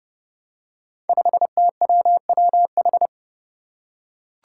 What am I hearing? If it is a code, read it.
Morse code, heard as a keyed tone, about 30 wpm, 700 Hz. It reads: 5TWWH
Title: Five Challenging Letter-Number Combinations 30wpm